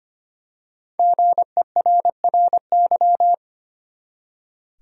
Morse code, heard as a keyed tone, about 25 wpm, 700 Hz. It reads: GERRY
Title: USA Common Names - 25wpm